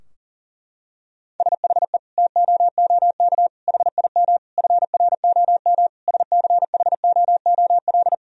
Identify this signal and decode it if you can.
Morse code, heard as a keyed tone, about 40 wpm, 700 Hz. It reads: SHE TOOK HIM FROM SCHOOL
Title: Sentences from Top 1000 Words - Review Part 4 - 40wpm